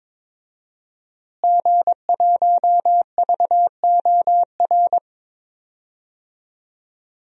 Morse code, heard as a keyed tone, about 22 wpm, 700 Hz. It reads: G1VOR
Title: DXCC-100 Call Signs - 5 Characters - Repeated 2x - 22wpm